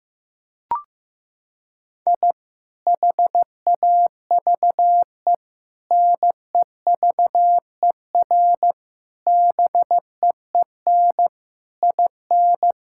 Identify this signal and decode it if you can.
Morse code, heard as a keyed tone, about 15 wpm, 700 Hz. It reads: I HAVE NEVER BEEN IN
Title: Sentences from Top 1000 Words - Review Part 5 - 15wpm